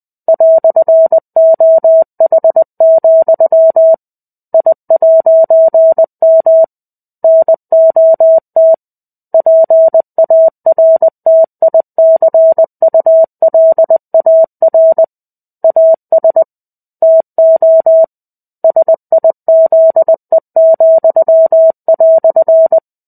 Morse code, heard as a keyed tone, about 20 wpm, 650 Hz. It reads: "OH, I'M NOT PARTICULAR AS TO SIZE,"
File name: 05-Chapter-05 20 WPM